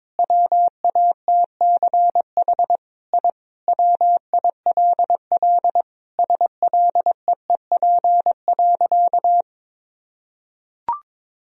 Morse code, heard as a keyed tone, about 22 wpm, 700 Hz. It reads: WATCH I WILL SLEEP.
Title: Sentences from Top 1000 Words - Review Part 2 - 22wpm